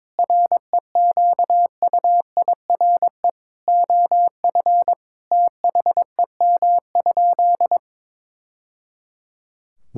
Morse code, heard as a keyed tone, about 22 wpm, 700 Hz. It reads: REQUIRE OF THEM?
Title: Sentences from Top 1000 Words - Review Part 6 - 22wpm